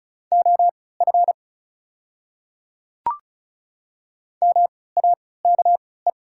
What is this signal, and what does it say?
Morse code, heard as a keyed tone, about 34 wpm, spaced as for 20 wpm, 700 Hz. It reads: OF MAKE
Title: 3x Character Spacing - Top 500 Words - 35wpm